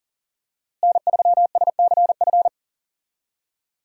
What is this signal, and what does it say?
Morse code, heard as a keyed tone, about 39 wpm, 700 Hz. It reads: N3SCF
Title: USA Call Signs - 5 Characters - Repeated 2x - 40wpm